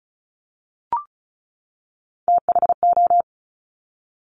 Morse code, heard as a keyed tone, about 36 wpm, 700 Hz. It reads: THO
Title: Top Three Letter Combinations in Words 35wpm